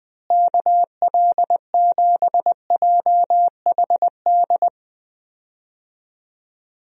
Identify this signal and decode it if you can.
Morse code, heard as a keyed tone, about 21 wpm, 700 Hz. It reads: KL7JHD
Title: USA Call Signs - 6 Characters - Repeated 3x - 20wpm